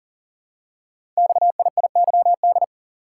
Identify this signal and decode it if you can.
Morse code, heard as a keyed tone, about 40 wpm, 700 Hz. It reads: XIIYD
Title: Code Groups - 5 Letter:Numbers - 40wpm